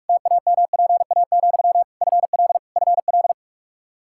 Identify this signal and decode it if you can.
Morse code, heard as a keyed tone, about 46 wpm, 700 Hz. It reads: TAMPA, FL FL
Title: K1USN SST QSO Practice 45wpm